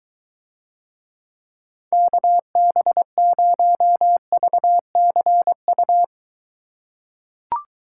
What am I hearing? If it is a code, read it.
Morse code, heard as a keyed tone, about 23 wpm, 700 Hz. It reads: KB0VCU